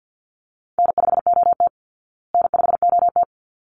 Morse code, heard as a keyed tone, about 50 wpm, 700 Hz. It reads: N5OT N5OT
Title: Easy Call Signs Repeated 2x - 50wpm